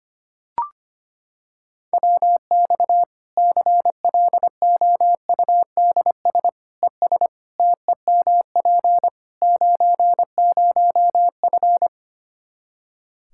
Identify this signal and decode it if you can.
Morse code, heard as a keyed tone, about 25 wpm, 700 Hz. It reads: WX CLOUDS ES TEMP 90F
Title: Practice QSO - Exchange 2 - Info - 25wpm